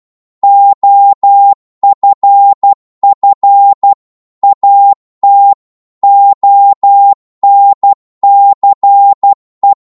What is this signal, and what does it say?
Morse code, heard as a keyed tone, about 12 wpm, 800 Hz. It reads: OFF AT ONCE